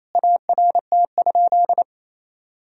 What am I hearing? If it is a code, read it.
Morse code, heard as a keyed tone, about 28 wpm, 700 Hz. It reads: ART?
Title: Sentences from Top 1000 Words - Review Part 3 - 28wpm